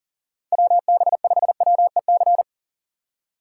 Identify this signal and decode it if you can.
Morse code, heard as a keyed tone, about 40 wpm, 700 Hz. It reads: WB5WEC
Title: K1USN SST QSO Practice - 40wpm